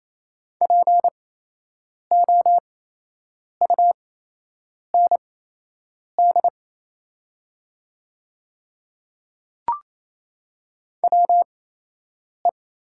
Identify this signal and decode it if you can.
Morse code, heard as a keyed tone, about 28 wpm, spaced as for 8 wpm, 700 Hz. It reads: POUND WE
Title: 8x Character Spacing - Top 500 Words - 28wpm